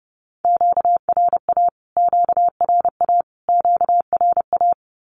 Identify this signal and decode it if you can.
Morse code, heard as a keyed tone, about 30 wpm, 700 Hz. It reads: QRA QRA QRA
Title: Q-Codes Repeated 3x 30wpm